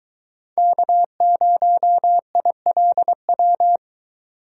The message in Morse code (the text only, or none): K0ILW